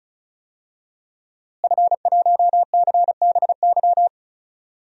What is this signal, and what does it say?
Morse code, heard as a keyed tone, about 34 wpm, 700 Hz. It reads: F1CBY
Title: DXCC-100 Call Signs - 5 Characters - Repeated 3x - 35wpm